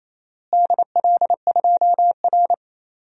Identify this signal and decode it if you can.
Morse code, heard as a keyed tone, about 27 wpm, 700 Hz. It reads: DL2R
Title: DXCC-100 Call Signs - 4 Characters - Repeated 3x - 28wpm